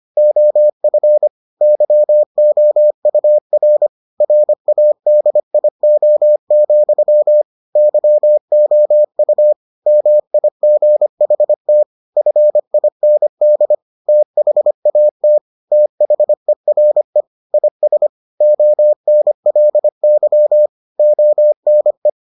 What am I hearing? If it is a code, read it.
Morse code, heard as a keyed tone, about 25 wpm, 600 Hz. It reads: OF YOUR RADIO, YOU MIGHT FIND THAT THERE IS ONLY ONE